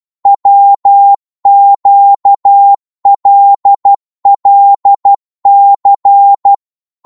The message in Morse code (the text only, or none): WQLLC